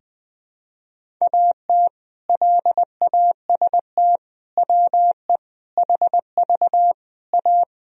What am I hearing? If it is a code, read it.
Morse code, heard as a keyed tone, about 21 wpm, 700 Hz. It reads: AT LAST WE HV A
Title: Sentences from Top 300 Words and Abbreviations 20wpm